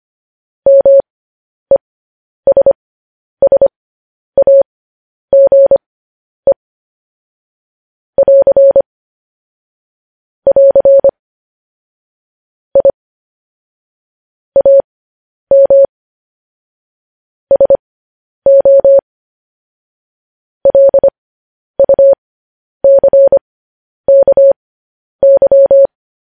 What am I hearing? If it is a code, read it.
Morse code, heard as a keyed tone, about 25 wpm, spaced as for 10 wpm, 550 Hz. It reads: MESSAGE <AR> <AR> I AM SO LUCKY
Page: News Headlines in Morse Code at 10 WPM